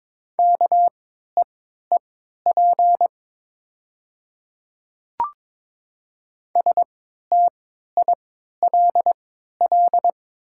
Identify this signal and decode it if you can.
Morse code, heard as a keyed tone, about 22 wpm, spaced as for 12 wpm, 700 Hz. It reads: KEEP STILL
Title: 3x Character Spacing - Top 500 Words - 22wpm